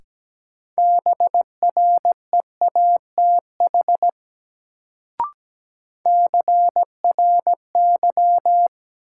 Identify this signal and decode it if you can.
Morse code, heard as a keyed tone, about 17 wpm, 700 Hz. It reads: BREATH CRY
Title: Sets of 5 Words Encore - New Word Drill - 17wpm